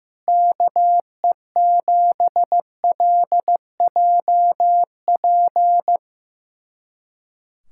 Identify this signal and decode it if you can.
Morse code, heard as a keyed tone, about 15 wpm, 700 Hz. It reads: KE7LJP
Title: USA Callsign DE Callsign 15wpm